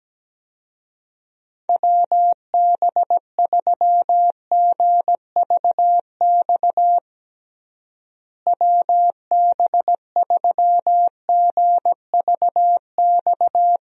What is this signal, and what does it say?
Morse code, heard as a keyed tone, about 17 wpm, 700 Hz. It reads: WB3GVX WB3GVX
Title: USA Call Signs - 6 Characters - Repeated 2x - 17wpm